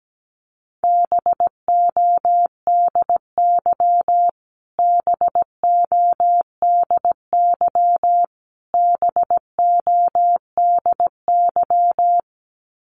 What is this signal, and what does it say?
Morse code, heard as a keyed tone, about 17 wpm, 700 Hz. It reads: BODY BODY BODY
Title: Top 500 Words Repeated 17wpm